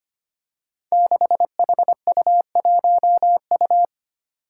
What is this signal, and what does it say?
Morse code, heard as a keyed tone, about 25 wpm, 700 Hz. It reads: 6HU1U
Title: Five Challenging Letter-Number Combinations 25wpm